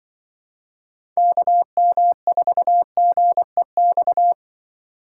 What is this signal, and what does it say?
Morse code, heard as a keyed tone, about 24 wpm, 700 Hz. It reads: KM4GEX
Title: Speed Racing - USA 2x3 Call Signs 20wpm